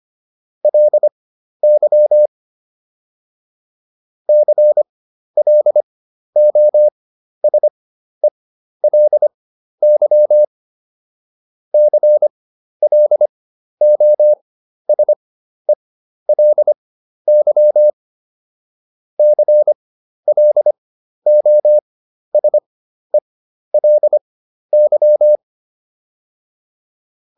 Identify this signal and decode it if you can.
Morse code, heard as a keyed tone, about 25 wpm, spaced as for 12 wpm, 600 Hz. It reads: LY CLOSELY CLOSELY CLOSELY